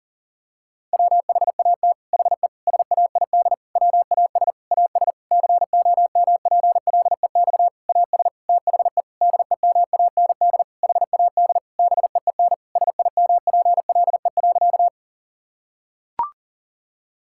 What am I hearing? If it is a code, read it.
Morse code, heard as a keyed tone, about 40 wpm, 700 Hz. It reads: WHAT HE SAID WAS AS COMPLEX AS THE DEMAND HAD BEEN SIMPLE.
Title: Sentences from Top 1000 Words - Review Part 6 - 40wpm